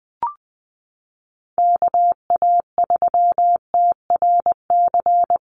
Morse code, heard as a keyed tone, about 20 wpm, 700 Hz. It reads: KA3TRC
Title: USA Call Signs Repeated 3x 20wpm